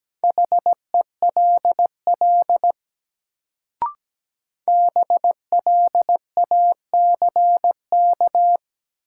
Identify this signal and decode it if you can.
Morse code, heard as a keyed tone, about 17 wpm, 700 Hz. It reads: HELL BLACK
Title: Sets of 5 Words Encore - New Word Drill - 17wpm